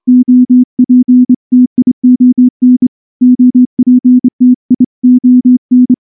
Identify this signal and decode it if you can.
Morse code, heard as a keyed tone, about 25 wpm, 250 Hz. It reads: OPTION OPTION
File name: morse / morse-training-25WPM